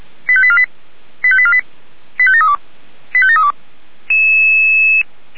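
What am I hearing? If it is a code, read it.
Five-tone selective call (ZVEI), heard as quick runs of five tones, tones of about 70 ms. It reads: 85848 85848 85732 85732
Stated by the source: Five-tone ZVEI selective call received on SDR.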